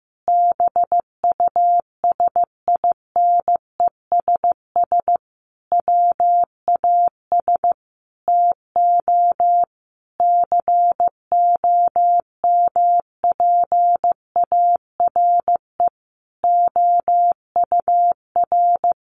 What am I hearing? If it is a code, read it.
Morse code, heard as a keyed tone, about 15 wpm, 700 Hz. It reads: BUSINESS WAS TO COMPARE OUR
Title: Sentences from Top 1300 Words 15wpm